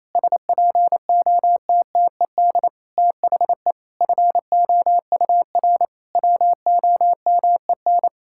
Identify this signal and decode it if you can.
Morse code, heard as a keyed tone, about 28 wpm, 700 Hz. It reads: SPOTTED THE FOUR WOMEN
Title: Pangram Challenge v2 - 28wpm